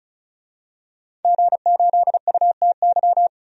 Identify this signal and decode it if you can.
Morse code, heard as a keyed tone, about 36 wpm, 700 Hz. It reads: G8UTY
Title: DXCC-100 Call Signs - 5 Characters - 35wpm